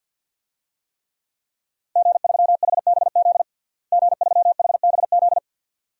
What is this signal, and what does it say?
Morse code, heard as a keyed tone, about 46 wpm, 700 Hz. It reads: G3HBZ G3HBZ
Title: DXCC-100 Call Signs - 5 Characters - Repeated 2x - 50wpm